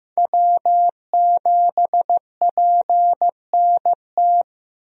Morse code, heard as a keyed tone, about 15 wpm, 700 Hz. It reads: W7PNT
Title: USA Call Signs - 5 Characters - Repeated 2x - 15wpm